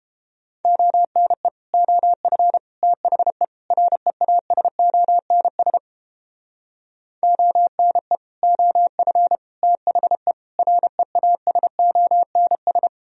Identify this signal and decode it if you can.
Morse code, heard as a keyed tone, about 31 wpm, 700 Hz. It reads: ONE OF THE REASONS ONE OF THE REASONS